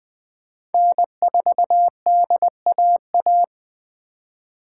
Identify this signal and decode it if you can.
Morse code, heard as a keyed tone, about 20 wpm, 700 Hz. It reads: N4DAA
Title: Speed Racing - USA Call Signs 17wpm